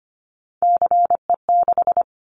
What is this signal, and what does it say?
Morse code, heard as a keyed tone, about 25 wpm, 700 Hz. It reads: CE6
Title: Callsign Prefixes for Top 100 DXCC - 25wpm